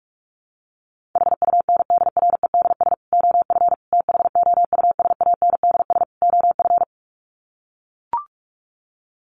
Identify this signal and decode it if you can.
Morse code, heard as a keyed tone, about 44 wpm, 700 Hz. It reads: HUNDREDS OF THOUSANDS OF
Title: Sets of 4 Words 45wpm